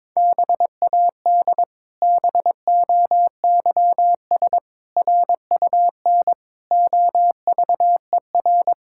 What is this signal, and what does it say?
Morse code, heard as a keyed tone, about 21 wpm, 700 Hz. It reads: BAD BOYS RUN OVER
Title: Pangram Challenge v2 - 22wpm